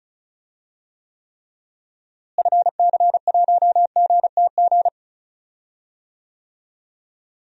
Morse code, heard as a keyed tone, about 36 wpm, 700 Hz. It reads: FC1GTG
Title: DXCC-100 Call Signs - 6 Characters - Repeated 3x - 35wpm